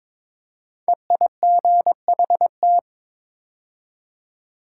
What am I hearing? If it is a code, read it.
Morse code, heard as a keyed tone, about 22 wpm, 700 Hz. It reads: EIGHT
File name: 5 Letter Words - First 3 Letters Given - 22wpm